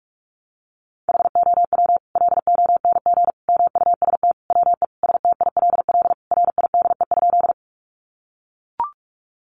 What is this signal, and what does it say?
Morse code, heard as a keyed tone, about 44 wpm, 700 Hz. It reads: HOW LONG MUST WE STILL RIDE?
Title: Sentences from Top 1200 Words 45wpm